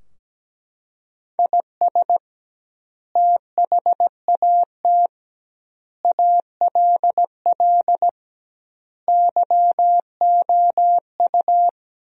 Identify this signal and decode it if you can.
Morse code, heard as a keyed tone, about 17 wpm, 700 Hz. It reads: IS THAT ALL YOU
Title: 2x Word-Spacing - Sentences from Top 200 Words - 17wpm